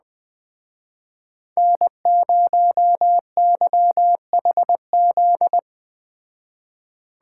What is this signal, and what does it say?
Morse code, heard as a keyed tone, about 22 wpm, 700 Hz. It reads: N0YHZ